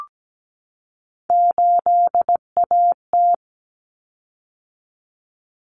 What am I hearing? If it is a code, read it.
Morse code, heard as a keyed tone, about 17 wpm, 690 Hz. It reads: 8AT
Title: Common Cut Numbers - 3-Digit 17wpm